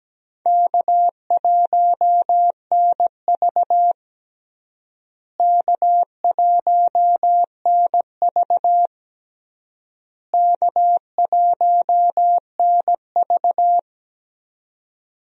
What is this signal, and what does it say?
Morse code, heard as a keyed tone, about 17 wpm, 700 Hz. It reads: K1NV K1NV K1NV
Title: USA Call Signs - 4 Characters - Repeated 3x - 17wpm